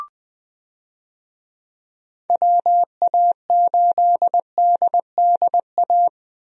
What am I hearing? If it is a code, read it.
Morse code, heard as a keyed tone, about 20 wpm, 700 Hz. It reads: WA8DDA
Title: USA Call Signs - 6 Characters - 20wpm